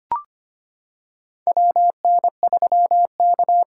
Morse code, heard as a keyed tone, about 25 wpm, 700 Hz. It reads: WN3K
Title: K1USN SST QSO Practice - 25wpm